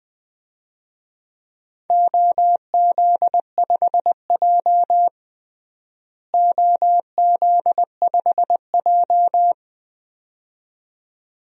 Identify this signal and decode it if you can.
Morse code, heard as a keyed tone, about 21 wpm, 700 Hz. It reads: OZ5J OZ5J
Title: DXCC-100 Call Signs - 4 Characters - Repeated 2x - 20wpm